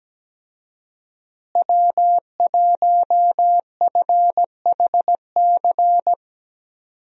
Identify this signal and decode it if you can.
Morse code, heard as a keyed tone, about 17 wpm, 700 Hz. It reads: W1FHC
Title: USA Call Signs - 5 Characters - Repeated 3x - 17wpm